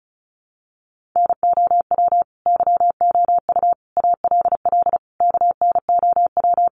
Morse code, heard as a keyed tone, about 34 wpm, 700 Hz. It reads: NOW YOU ALL KNOW
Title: Sentences from Top 1200 Words 35wpm